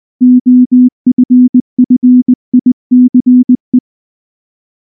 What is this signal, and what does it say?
Morse code, heard as a keyed tone, about 19 wpm, 250 Hz. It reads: OFFICE
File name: morse / morse-training-20WPM-10